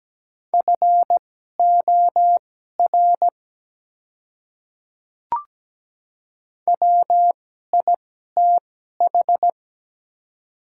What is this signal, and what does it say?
Morse code, heard as a keyed tone, about 17 wpm, spaced as for 12 wpm, 700 Hz. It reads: FOR WITH
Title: 2x Character Spacing - Top 100 Words - 17wpm